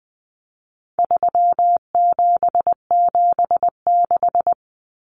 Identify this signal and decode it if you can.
Morse code, heard as a keyed tone, about 20 wpm, 700 Hz. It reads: 3776
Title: Numbers - 4-Digit 20wpm